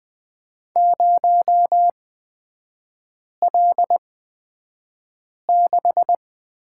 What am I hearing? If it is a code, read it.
Morse code, heard as a keyed tone, about 20 wpm, spaced as for 5 wpm, 700 Hz. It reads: 0L6
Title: 236 Lesson 38e - Letter q - Three Characters 20wpm